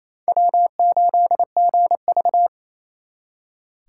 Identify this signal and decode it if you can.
Morse code, heard as a keyed tone, about 35 wpm, 700 Hz. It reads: W8GV
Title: Speed Racing - USA Call Signs 28wpm